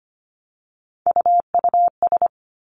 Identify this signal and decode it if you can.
Morse code, heard as a keyed tone, about 25 wpm, 700 Hz. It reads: UUS
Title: Three Challenging Letter-Number Combinations 25wpm